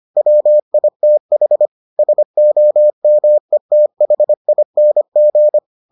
Morse code, heard as a keyed tone, about 25 wpm, 600 Hz. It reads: WITH SOMETHING